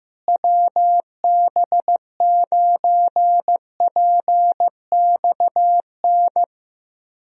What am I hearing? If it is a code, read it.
Morse code, heard as a keyed tone, about 15 wpm, 700 Hz. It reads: WB9PXN
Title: USA Call Signs - 6 Characters - Repeated 3x - 15wpm